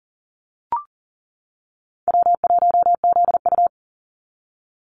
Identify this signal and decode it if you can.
Morse code, heard as a keyed tone, about 40 wpm, 700 Hz. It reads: W1ZU
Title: NA SOTA Activator Callsigns 40wpm